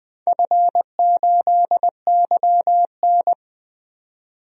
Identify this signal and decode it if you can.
Morse code, heard as a keyed tone, about 20 wpm, 700 Hz. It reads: F8YN
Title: DXCC-100 Call Signs - 4 Characters - 20wpm